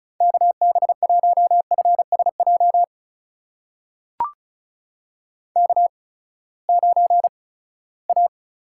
Morse code, heard as a keyed tone, about 36 wpm, spaced as for 20 wpm, 700 Hz. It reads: KB1FSJ K9A
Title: USA Call Signs - 8x Character Spacing - 35wpm